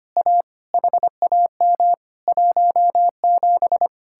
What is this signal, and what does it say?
Morse code, heard as a keyed tone, about 32 wpm, 700 Hz. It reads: A HAM 17